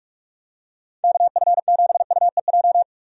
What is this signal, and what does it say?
Morse code, heard as a keyed tone, about 46 wpm, 700 Hz. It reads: KF7UEJ
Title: USA Call Signs - 6 Characters - 45wpm